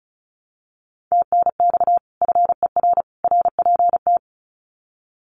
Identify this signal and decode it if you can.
Morse code, heard as a keyed tone, about 34 wpm, 700 Hz. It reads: TNX FER RPT
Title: Common QSO Phrases 35wpm